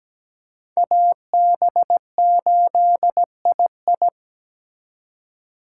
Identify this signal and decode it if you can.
Morse code, heard as a keyed tone, about 20 wpm, 700 Hz. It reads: AB8II
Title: Speed Racing - USA 2x2 Call Signs 17wpm